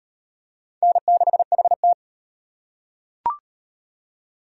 Morse code, heard as a keyed tone, about 38 wpm, 700 Hz. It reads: N6HT